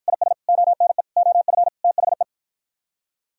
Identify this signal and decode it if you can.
Morse code, heard as a keyed tone, about 55 wpm, 700 Hz. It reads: IS ONE OF THE